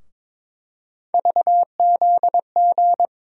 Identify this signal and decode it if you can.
Morse code, heard as a keyed tone, about 21 wpm, 700 Hz. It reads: VZG
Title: USA Call Sign Postfixes - 3 Letters 22wpm